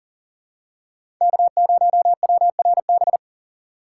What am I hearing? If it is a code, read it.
Morse code, heard as a keyed tone, about 40 wpm, 700 Hz. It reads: K0WRB
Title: USA Call Signs - 4x Character Spacing - 40wpm